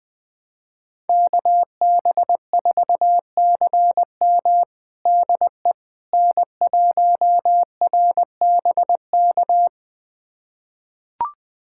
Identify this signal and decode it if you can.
Morse code, heard as a keyed tone, about 20 wpm, 700 Hz. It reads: KB4CM DE N1RBK
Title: USA Callsign DE Callsign - 5 Characters - 20wpm